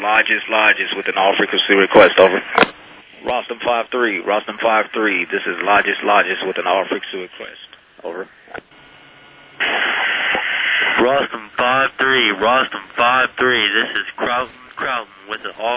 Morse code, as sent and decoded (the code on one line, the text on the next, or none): none